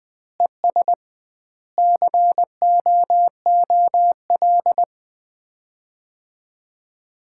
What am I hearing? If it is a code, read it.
Morse code, heard as a keyed tone, about 20 wpm, 700 Hz. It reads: ES COOL
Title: Common QSO Phrases - 2x Word-Spacing - 20wpm